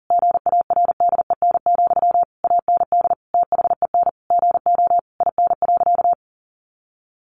Morse code, heard as a keyed tone, about 37 wpm, 700 Hz. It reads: GARDEN, AND THEN GO IN.
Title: Sentences from Top 700 Words 40wpm